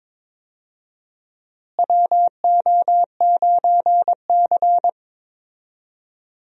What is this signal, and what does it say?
Morse code, heard as a keyed tone, about 22 wpm, 700 Hz. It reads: WO9C
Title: USA Call Signs - 4 Characters - Repeated 3x - 22wpm